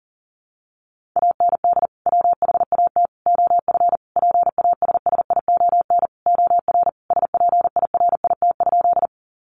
Morse code, heard as a keyed tone, about 40 wpm, 700 Hz. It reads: AND WHAT OF PASSION OR SPIRIT?
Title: Sentences from Top 1400 Words 40wpm